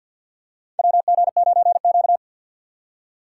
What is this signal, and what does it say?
Morse code, heard as a keyed tone, about 52 wpm, 700 Hz. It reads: WG9Q
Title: USA Call Signs - 3x Character Spacing - 50wpm